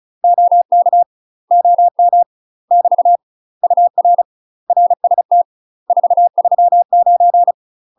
Morse code, heard as a keyed tone, about 34 wpm, 700 Hz. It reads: OK OM <BT> UR RST 439
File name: UK MASTER QSOs TRAINING / 35WPM / UK MASTER QSOs 0041